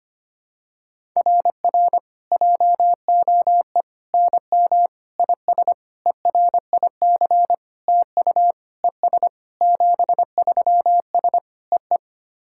Morse code, heard as a keyed tone, about 25 wpm, 700 Hz. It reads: RR JOE NM IS ERIC TU ES 73S EE
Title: SOTA QSO Practice 25wpm